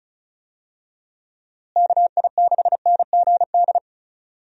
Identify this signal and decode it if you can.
Morse code, heard as a keyed tone, about 36 wpm, 700 Hz. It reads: KI6NGD
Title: USA Call Signs - 6 Characters - Repeated 3x - 35wpm